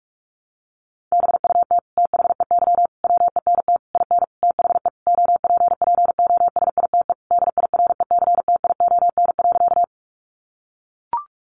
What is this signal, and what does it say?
Morse code, heard as a keyed tone, about 44 wpm, 700 Hz. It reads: BUT THEY WENT IN THE OPPOSITE DIRECTION.
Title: Sentences from Top 1300 Words 45wpm